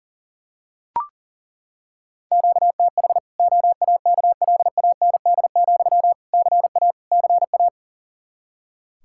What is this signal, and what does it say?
Morse code, heard as a keyed tone, about 40 wpm, 700 Hz. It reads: QTH OAKLAND, CA CA
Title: Practice QSO - Exchange 1 - Beginning - 40wpm